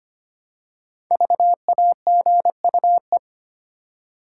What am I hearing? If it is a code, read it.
Morse code, heard as a keyed tone, about 25 wpm, 700 Hz. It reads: VAGUE
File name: 5 Letter Words - First 3 Letters Given - 25wpm